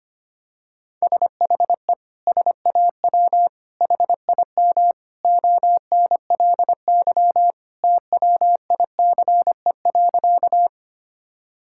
Morse code, heard as a keyed tone, about 25 wpm, 700 Hz. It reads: SHE SAW HIM ONLY TWICE.
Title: Sentences from Top 1600 Words 25wpm